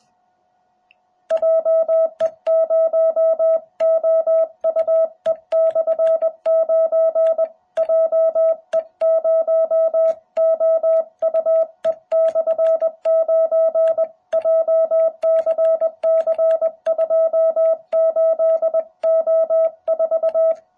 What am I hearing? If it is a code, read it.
Morse code, heard as a keyed tone, about 20 wpm, 650 Hz. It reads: JE0OUE/9 JE0OUE/9 JCC28O4